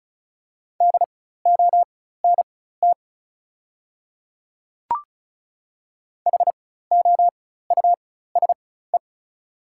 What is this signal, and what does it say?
Morse code, heard as a keyed tone, about 36 wpm, spaced as for 16 wpm, 690 Hz. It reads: DONT HOUSE
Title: 4x Character Spacing - Top 500 Words - 35wpm